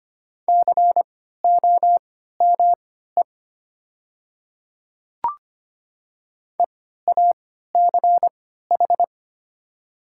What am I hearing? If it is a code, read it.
Morse code, heard as a keyed tone, about 25 wpm, spaced as for 14 wpm, 700 Hz. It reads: COME EACH
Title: 3x Character Spacing - Top 200 Words - 25wpm